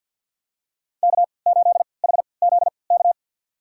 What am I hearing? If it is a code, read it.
Morse code, heard as a keyed tone, about 50 wpm, spaced as for 28 wpm, 700 Hz. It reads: K8HZK
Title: USA Call Signs - 3x Character Spacing - 50wpm